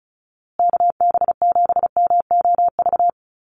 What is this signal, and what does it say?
Morse code, heard as a keyed tone, about 34 wpm, 700 Hz. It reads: KB7MOV
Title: USA Call Signs Repeated 3x 35wpm